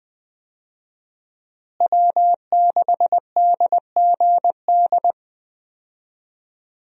Morse code, heard as a keyed tone, about 20 wpm, 700 Hz. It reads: W6DGD
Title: USA Call Signs - 5 Characters - Repeated 3x - 20wpm